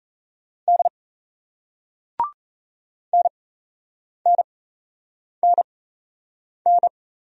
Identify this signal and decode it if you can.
Morse code, heard as a keyed tone, about 34 wpm, spaced as for 8 wpm, 700 Hz. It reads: D NNNN
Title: Speed-Racing Single Letters 28wpm